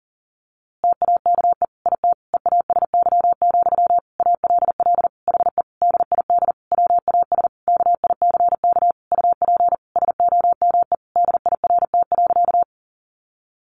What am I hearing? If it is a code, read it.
Morse code, heard as a keyed tone, about 40 wpm, 700 Hz. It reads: TAKE IT EASY, ALL HE DID WAS KICK UP SOME DIRT.
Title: A Sound Of Thunder 40wpm